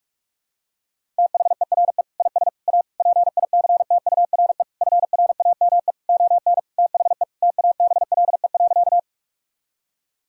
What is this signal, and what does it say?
Morse code, heard as a keyed tone, about 46 wpm, 700 Hz. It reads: THERE IS A PICTURE FRAME ON THE TABLE.
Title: Sentences from Top 1000 Words - Review Part 6 - 45wpm